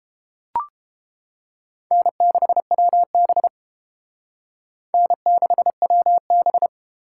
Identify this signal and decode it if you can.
Morse code, heard as a keyed tone, about 28 wpm, 700 Hz. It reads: N6WB N6WB